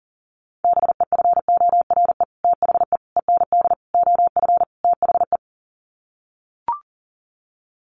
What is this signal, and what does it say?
Morse code, heard as a keyed tone, about 37 wpm, 700 Hz. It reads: BEFORE THE END OF THE
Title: Sets of 5 Words Encore 40wpm